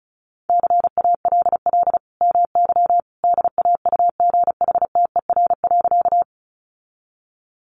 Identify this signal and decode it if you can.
Morse code, heard as a keyed tone, about 34 wpm, 700 Hz. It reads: CALL MY DAUGHTER.
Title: Sentences from Top 1500 Words 35wpm